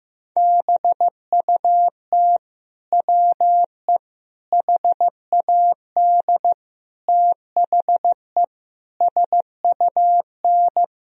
Morse code, heard as a keyed tone, about 15 wpm, 700 Hz. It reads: BUT WE HAD THE SUN
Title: Sentences from Top 1000 Words - Review Part 3 - 15wpm